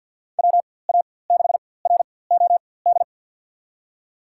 Morse code, heard as a keyed tone, about 53 wpm, spaced as for 24 wpm, 700 Hz. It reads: WA6ROD
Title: USA Call Signs - 6 Characters - 4x Character Spacing - 50wpm